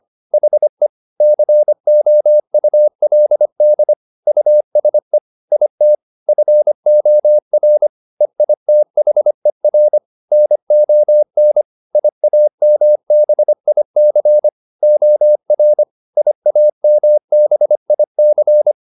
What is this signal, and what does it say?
Morse code, heard as a keyed tone, about 25 wpm, 600 Hz. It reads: HE COULD USE IT FOR EITHER NON IAMBIC OR IAMBIC